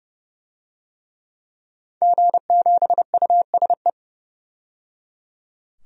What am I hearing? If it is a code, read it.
Morse code, heard as a keyed tone, about 29 wpm, 700 Hz. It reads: G7USE
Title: DXCC-100 Call Signs - 5 Characters - 30wpm